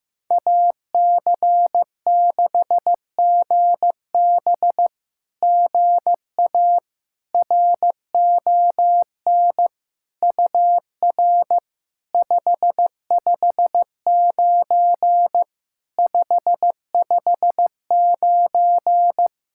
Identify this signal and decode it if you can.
Morse code, heard as a keyed tone, about 15 wpm, 700 Hz. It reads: AC6GB GA RON UR 559 559
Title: SOTA QSO Practice 15wpm